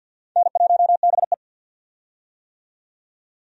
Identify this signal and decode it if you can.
Morse code, heard as a keyed tone, about 46 wpm, 700 Hz. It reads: N1BE
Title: USA Call Signs - 4 Characters - 50wpm